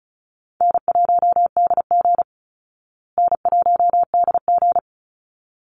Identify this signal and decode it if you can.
Morse code, heard as a keyed tone, about 34 wpm, 700 Hz. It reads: N1DG N1DG
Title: Call Signs Repeated 3x - 35wpm